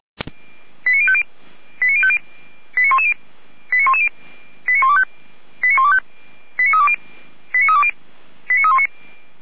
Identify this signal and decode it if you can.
Five-tone selective call (ZVEI), heard as quick runs of five tones, tones of about 70 ms. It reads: 89050 89110 89110 89125 89125 89320 89319